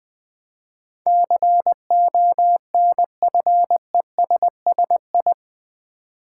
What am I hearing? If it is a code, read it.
Morse code, heard as a keyed tone, about 20 wpm, 700 Hz. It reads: CONFESSI
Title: Build A Word - 4 Letters At A Time - 10-Letter Words - 20wpm